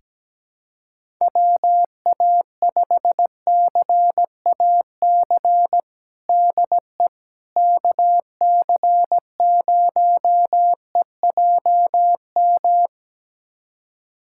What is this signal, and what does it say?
Morse code, heard as a keyed tone, about 17 wpm, 700 Hz. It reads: WA5CAC DE KC0EJM
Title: USA Callsign DE Callsign 17wpm